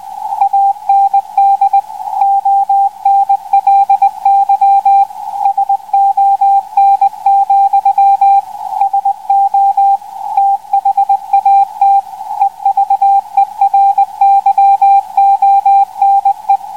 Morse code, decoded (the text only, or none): AND ONLY SON, SO THAT EVERYONE